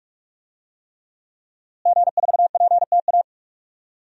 Morse code, heard as a keyed tone, about 43 wpm, 700 Hz. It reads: G4PTA
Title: DXCC-100 Call Signs - 5 Characters - Repeated 3x - 45wpm